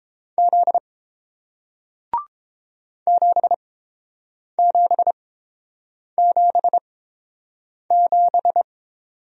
Morse code, heard as a keyed tone, about 30 wpm, spaced as for 7 wpm, 700 Hz. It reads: Z 7777